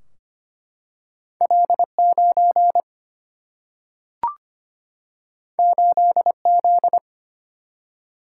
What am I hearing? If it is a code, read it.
Morse code, heard as a keyed tone, about 25 wpm, 700 Hz. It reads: L9 8Z
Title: Single-Letter Single-Number 25wpm